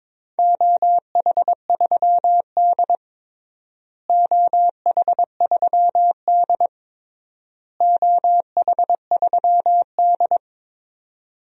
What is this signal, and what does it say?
Morse code, heard as a keyed tone, about 22 wpm, 700 Hz. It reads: OH3D OH3D OH3D
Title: DXCC-100 Call Signs - 4 Characters - Repeated 3x - 22wpm